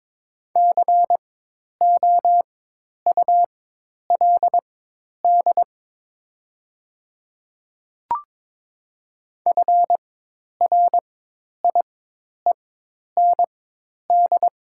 Morse code, heard as a keyed tone, about 22 wpm, spaced as for 10 wpm, 700 Hz. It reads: COULD FRIEND
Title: 4x Character Spacing - Top 500 Words - 22wpm